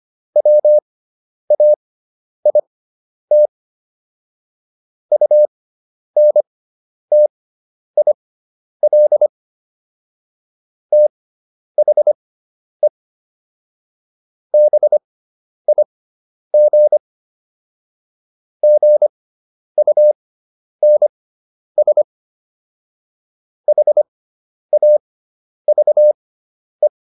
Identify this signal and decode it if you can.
Morse code, heard as a keyed tone, about 25 wpm, spaced as for 10 wpm, 600 Hz. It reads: WAIT UNTIL THE BIG GUNS HAVE